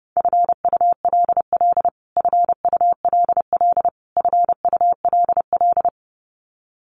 Morse code, heard as a keyed tone, about 30 wpm, 700 Hz. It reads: FULL FULL FULL
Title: Top 500 Words Repeated 30wpm